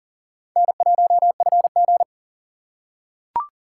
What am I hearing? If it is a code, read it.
Morse code, heard as a keyed tone, about 39 wpm, 700 Hz. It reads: N1FG
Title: USA Call Signs - 4 Characters - 6x Character Spacing - 40wpm